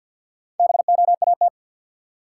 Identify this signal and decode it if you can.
Morse code, heard as a keyed tone, about 53 wpm, 700 Hz. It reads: BOAT
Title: Sentences from Top 500 Words - New Word Drill - 50wpm